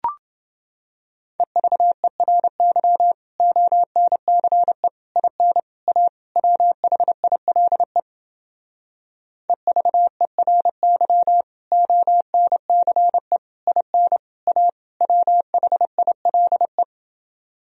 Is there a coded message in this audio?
Yes: Morse code from 1.4 to 16.8 s, heard as a keyed tone, about 29 wpm, 700 Hz. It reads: EVERY ONCE IN A WHILE EVERY ONCE IN A WHILE